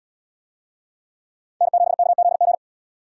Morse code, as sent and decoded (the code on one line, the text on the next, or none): -. --... -.- --.. --.
N7KZG